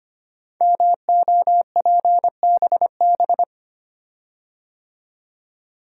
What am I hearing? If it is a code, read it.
Morse code, heard as a keyed tone, about 25 wpm, 700 Hz. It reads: MOPBB
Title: Code Groups - 5 Letters - 25wpm